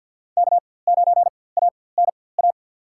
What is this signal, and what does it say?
Morse code, heard as a keyed tone, about 52 wpm, spaced as for 24 wpm, 700 Hz. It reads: K9ANA
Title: USA Call Signs - 5 Characters - 4x Character Spacing - 50wpm